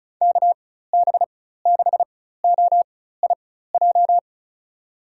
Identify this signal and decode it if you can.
Morse code, heard as a keyed tone, about 34 wpm, spaced as for 16 wpm, 700 Hz. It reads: KB6OIJ
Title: USA Call Signs - 6 Characters - 4x Character Spacing - 35wpm